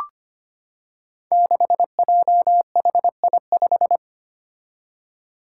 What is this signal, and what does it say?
Morse code, heard as a keyed tone, about 25 wpm, 700 Hz. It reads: 6JHI5
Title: Five Challenging Letter-Number Combinations 25wpm